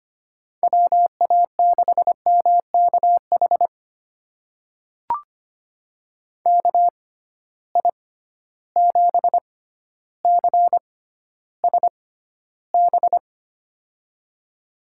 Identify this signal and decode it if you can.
Morse code, heard as a keyed tone, about 25 wpm, spaced as for 13 wpm, 700 Hz. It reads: WA6MKH KI7CSB
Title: USA Call Signs - 6 Characters - 6x Character Spacing - 25wpm